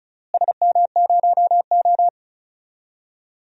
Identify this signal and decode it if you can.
Morse code, heard as a keyed tone, about 36 wpm, 700 Hz. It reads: SM0O
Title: DXCC-100 Call Signs - 4 Characters - Repeated 2x - 35wpm